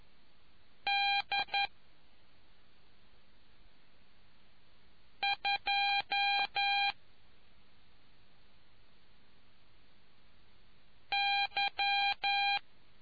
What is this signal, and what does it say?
Morse code, heard as a keyed tone, about 11 wpm, spaced as for 2 wpm, 800 Hz. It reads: D2Y